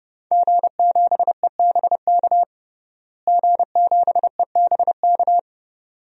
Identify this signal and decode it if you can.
Morse code, heard as a keyed tone, about 29 wpm, 700 Hz. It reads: G7EBK G7EBK
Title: DXCC-100 Call Signs - 5 Characters - Repeated 2x - 30wpm